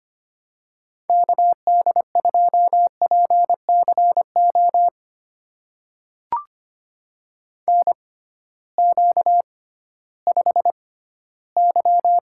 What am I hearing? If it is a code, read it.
Morse code, heard as a keyed tone, about 25 wpm, spaced as for 14 wpm, 700 Hz. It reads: KD2PCO NQ5Y
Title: USA Call Signs - 6x Character Spacing - 25wpm